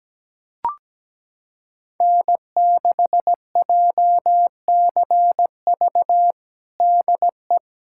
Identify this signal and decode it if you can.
Morse code, heard as a keyed tone, about 17 wpm, 700 Hz. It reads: N6JCV DE
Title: USA Callsign DE Callsign - 5 Characters - 17wpm